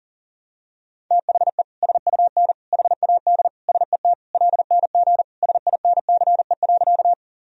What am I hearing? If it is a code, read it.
Morse code, heard as a keyed tone, about 40 wpm, 700 Hz. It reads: THE SUN HAD SET LNG SINCE.
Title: Sentences from Top 300 Words and Abbreviations 40wpm